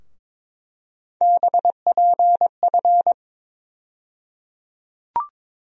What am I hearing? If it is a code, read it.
Morse code, heard as a keyed tone, about 22 wpm, 700 Hz. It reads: BPF
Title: USA Call Sign Postfixes - 3 Letters 22wpm